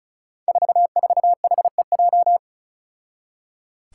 Morse code, heard as a keyed tone, about 34 wpm, 700 Hz. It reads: 44HEJ
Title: Five Challenging Letter-Number Combinations 35wpm